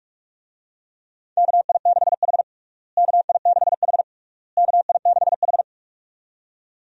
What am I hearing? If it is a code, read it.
Morse code, heard as a keyed tone, about 46 wpm, 700 Hz. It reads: KI6H KI6H KI6H
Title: USA Call Signs - 4 Characters - Repeated 3x - 45wpm